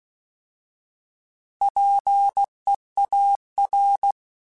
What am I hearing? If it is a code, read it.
Morse code, heard as a keyed tone, about 16 wpm, 800 Hz. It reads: PEAR
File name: Alphabetical / Alphabetical mp3s / Ps @15